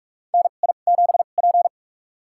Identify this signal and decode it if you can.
Morse code, heard as a keyed tone, about 44 wpm, spaced as for 32 wpm, 700 Hz. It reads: NI7P
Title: USA Call Signs - 2x Character Spacing - 45wpm